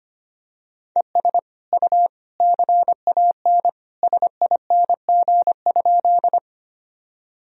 Morse code, heard as a keyed tone, about 25 wpm, 700 Hz. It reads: ES U CAN SING?
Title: Sentences from Top 400 Words and Abbreviations 25wpm